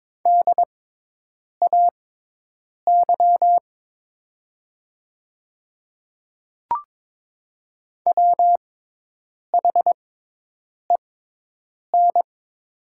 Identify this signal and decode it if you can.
Morse code, heard as a keyed tone, about 22 wpm, spaced as for 8 wpm, 700 Hz. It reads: DAY WHEN
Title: 6x Character Spacing - Top 100 Words - 22wpm